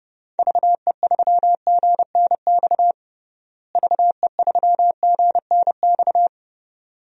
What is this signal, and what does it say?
Morse code, heard as a keyed tone, about 30 wpm, 700 Hz. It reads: VE3GNX VE3GNX
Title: DXCC-100 Call Signs - 6 Characters - Repeated 2x - 30wpm